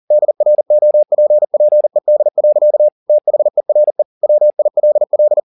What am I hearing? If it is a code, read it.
Morse code, heard as a keyed tone, about 40 wpm, 600 Hz. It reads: DROPPED. THERE WILL
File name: the old man and the sea 2 40wpm